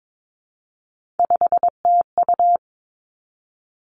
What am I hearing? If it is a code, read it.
Morse code, heard as a keyed tone, about 22 wpm, 700 Hz. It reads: 5TU
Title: Three Challenging Letter-Number Combinations 22wpm